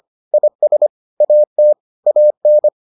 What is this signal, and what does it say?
Morse code, heard as a keyed tone, about 25 wpm, 600 Hz. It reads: IS AT AN